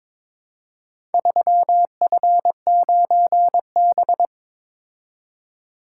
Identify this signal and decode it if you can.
Morse code, heard as a keyed tone, about 22 wpm, 700 Hz. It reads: 3F9B
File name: practice / 22wpm / Four Letter-Number Combinations - 22wpm